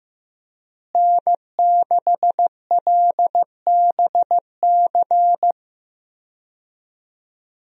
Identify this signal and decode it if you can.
Morse code, heard as a keyed tone, about 15 wpm, 700 Hz. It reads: N6LBC
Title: USA Call Signs - 5 Characters - Repeated 3x - 15wpm